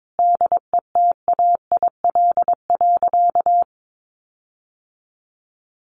Morse code, heard as a keyed tone, about 22 wpm, 700 Hz. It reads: DETAIL.
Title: Sentences from Top 1500 Words 22wpm